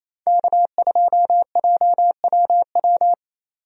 Morse code, heard as a keyed tone, about 28 wpm, 700 Hz. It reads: K2JWW
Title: USA Call Signs - 5 Characters - 28wpm